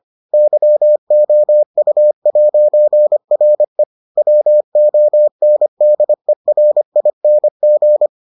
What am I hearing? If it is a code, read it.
Morse code, heard as a keyed tone, about 25 wpm, 600 Hz. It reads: YOU'RE WONDERING